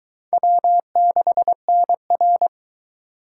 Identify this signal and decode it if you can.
Morse code, heard as a keyed tone, about 23 wpm, 700 Hz. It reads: W6NR